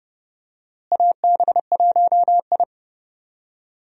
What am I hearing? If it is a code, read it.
Morse code, heard as a keyed tone, about 32 wpm, 700 Hz. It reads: AB1I
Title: USA Call Signs - 4 Characters - Repeated 3x - 30wpm